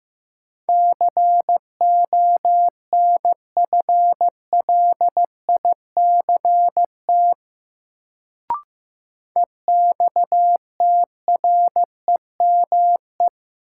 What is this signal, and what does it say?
Morse code, heard as a keyed tone, about 16 wpm, 700 Hz. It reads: CONFLICT EXTREME
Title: Sentences from Top 1300 Words - New Word Drill - 15wpm